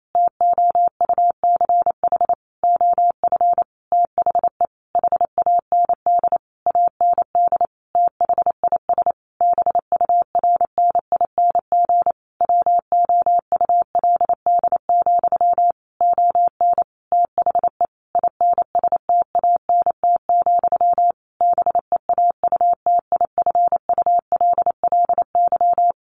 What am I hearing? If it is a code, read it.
Morse code, heard as a keyed tone, about 28 wpm, 700 Hz. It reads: TOUCH OF THE HAND AND THIS BURNING WOULD, ON THE INSTANT, BEAUTIFULLY
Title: A Sound Of Thunder 28wpm